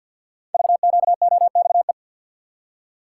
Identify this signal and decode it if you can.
Morse code, heard as a keyed tone, about 48 wpm, 700 Hz. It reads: VQOXE
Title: Code Groups - 5 Letters - 50wpm